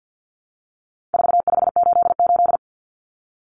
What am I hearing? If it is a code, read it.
Morse code, heard as a keyed tone, about 50 wpm, 700 Hz. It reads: 4588
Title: Common Cut Numbers - 4-Digit 50wpm